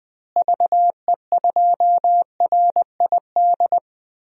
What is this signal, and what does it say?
Morse code, heard as a keyed tone, about 20 wpm, 700 Hz. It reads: VE2RID
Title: DXCC-100 Call Signs - 6 Characters - Repeated 3x - 20wpm